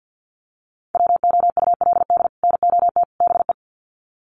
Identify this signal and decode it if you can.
Morse code, heard as a keyed tone, about 50 wpm, 700 Hz. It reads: WOULD NOT BE
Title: Sets of 3 Words 50wpm